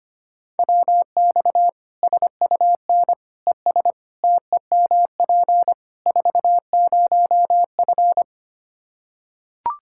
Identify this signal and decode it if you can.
Morse code, heard as a keyed tone, about 25 wpm, 700 Hz. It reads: WX SUN ES TEMP 40F
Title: Practice QSO - Exchange 2 - Info - 25wpm